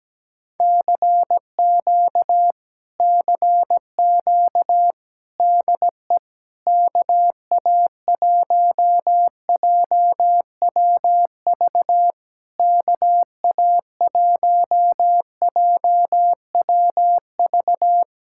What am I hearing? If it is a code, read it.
Morse code, heard as a keyed tone, about 17 wpm, 700 Hz. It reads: CQ CQ DE KA1JWV KA1JWV
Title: Speed-Racing Common QSO Phrases - 17wpm